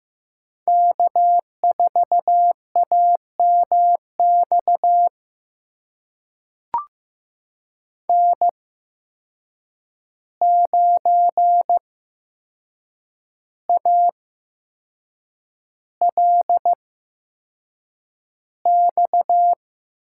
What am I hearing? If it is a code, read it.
Morse code, heard as a keyed tone, about 15 wpm, spaced as for 6 wpm, 700 Hz. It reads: K4AMX N9ALX
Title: USA Call Signs - 5 Characters - 8x Character Spacing - 15wpm